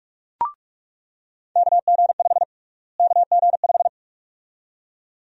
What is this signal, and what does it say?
Morse code, heard as a keyed tone, about 46 wpm, 700 Hz. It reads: KG5 KG5
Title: USA Call Sign Prefixes - Repeated 2x - 45wpm